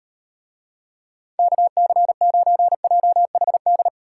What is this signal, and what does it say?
Morse code, heard as a keyed tone, about 38 wpm, 700 Hz. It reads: KC9JHD